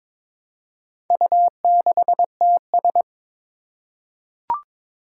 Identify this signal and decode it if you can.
Morse code, heard as a keyed tone, about 23 wpm, 700 Hz. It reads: U6TS
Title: Four Challenging Letter-Number Combinations 22wpm